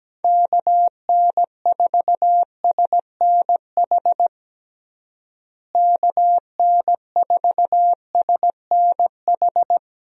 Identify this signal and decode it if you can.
Morse code, heard as a keyed tone, about 17 wpm, 700 Hz. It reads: KN4SNH KN4SNH
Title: USA Call Signs - 6 Characters - Repeated 2x - 17wpm